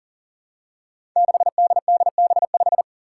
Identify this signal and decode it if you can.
Morse code, heard as a keyed tone, about 39 wpm, 700 Hz. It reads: 6DDB5
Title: Five Challenging Letter-Number Combinations 40wpm